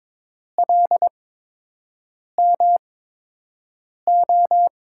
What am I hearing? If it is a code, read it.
Morse code, heard as a keyed tone, about 22 wpm, spaced as for 6 wpm, 700 Hz. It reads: LMO
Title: USA Call Signs - 5 Characters - 8x Character Spacing - 22wpm